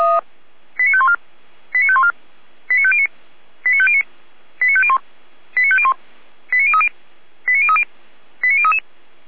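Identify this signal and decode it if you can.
Five-tone selective call (ZVEI), heard as quick runs of five tones, tones of about 70 ms. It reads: 89514 89514 89509 89509 89501 89501 89030 89030 89033
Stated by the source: ZVEI five-tone call 89080 received by SDR.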